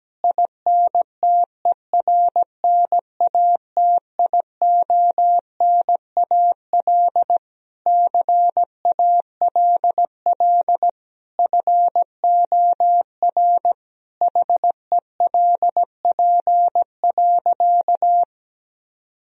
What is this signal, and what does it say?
Morse code, heard as a keyed tone, about 17 wpm, 700 Hz. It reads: INTERNATIONAL CALL FOR HELP.
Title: Sentences from Top 1000 Words - Review Part 4 - 17wpm